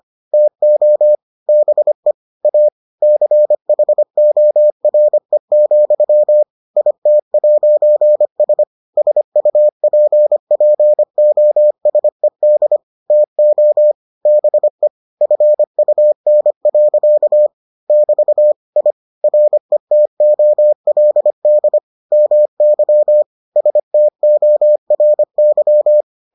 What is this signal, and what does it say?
Morse code, heard as a keyed tone, about 25 wpm, 600 Hz. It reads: TO BE A CHORE, IT'S SUPPOSED TO BE FUN. <BT> I RETOLD MY STORY